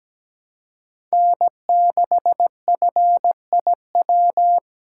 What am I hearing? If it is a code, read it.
Morse code, heard as a keyed tone, about 20 wpm, 700 Hz. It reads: N6FIW
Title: Speed Racing - USA 1x3 Call Signs 17wpm